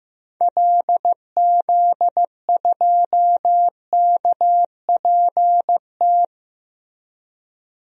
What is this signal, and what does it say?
Morse code, heard as a keyed tone, about 15 wpm, 700 Hz. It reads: LZ2KPT
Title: DXCC-100 Call Signs - 6 Characters - Repeated 3x - 15wpm